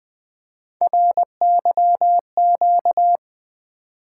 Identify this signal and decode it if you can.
Morse code, heard as a keyed tone, about 20 wpm, 700 Hz. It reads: RYQ
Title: Three Letter-Number Combinations - 20wpm